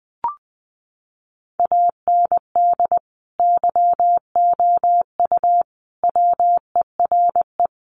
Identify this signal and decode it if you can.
Morse code, heard as a keyed tone, about 20 wpm, 700 Hz. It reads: AND YOU WERE
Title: Sentences from Top 1400 Words 20wpm